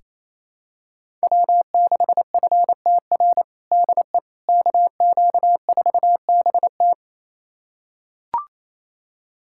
Morse code, heard as a keyed tone, about 27 wpm, 700 Hz. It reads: W6FTR DE KQ4BT
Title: USA Callsign DE Callsign - 5 Characters - 28wpm